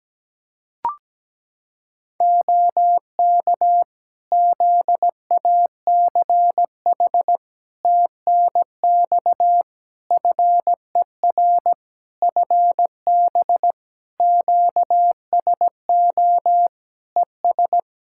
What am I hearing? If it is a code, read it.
Morse code, heard as a keyed tone, about 17 wpm, 700 Hz. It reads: OK ZACH TNX FER FB QSO ES
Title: K1USN SST QSO Practice 17wpm